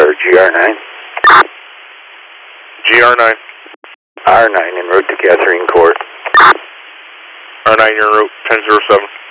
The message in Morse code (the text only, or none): none